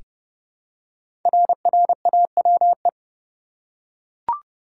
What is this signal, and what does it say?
Morse code, heard as a keyed tone, about 30 wpm, 700 Hz. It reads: RRAWE
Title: Code Groups - 5 Letters - 30wpm